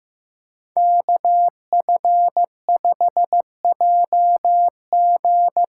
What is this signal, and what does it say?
Morse code, heard as a keyed tone, about 15 wpm, 700 Hz. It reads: KF5JG
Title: USA Call Signs - 6 Characters - 8x Character Spacing - 15wpm